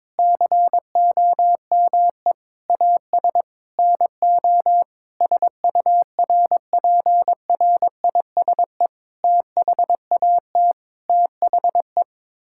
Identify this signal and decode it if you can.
Morse code, heard as a keyed tone, about 22 wpm, 700 Hz. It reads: COME AS NO SURPRISE THAT THE
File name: practice / 22wpm / Sets of 8 Words - 22wpm